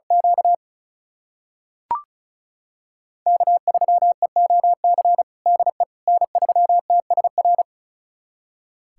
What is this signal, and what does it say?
Morse code, heard as a keyed tone, about 36 wpm, 700 Hz. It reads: Q K3EOC DE N3TSR
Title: USA Callsign DE Callsign - 5 Characters - 35wpm